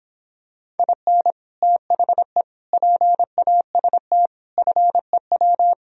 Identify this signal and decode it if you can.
Morse code, heard as a keyed tone, about 26 wpm, 700 Hz. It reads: IN THE PAST FEW